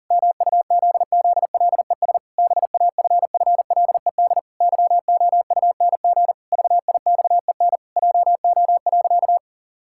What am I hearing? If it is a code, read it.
Morse code, heard as a keyed tone, about 37 wpm, 700 Hz. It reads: MUZZLES BAFFLED YOUNG VIXEN JO.
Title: Pangram Challenge v2 - 40wpm